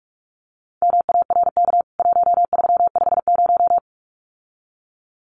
Morse code, heard as a keyed tone, about 44 wpm, 700 Hz. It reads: MARK 1350
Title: CWT Exchange 45wpm